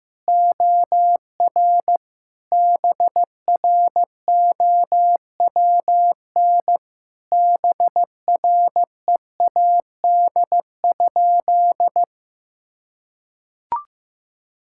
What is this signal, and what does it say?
Morse code, heard as a keyed tone, about 15 wpm, 700 Hz. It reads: OR BROWN BREAD?
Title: Sentences from Top 1600 Words 15wpm